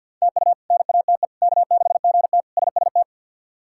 Morse code, heard as a keyed tone, about 47 wpm, 700 Hz. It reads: TU NATE K6GT SST
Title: K1USN SST QSO Practice - 50wpm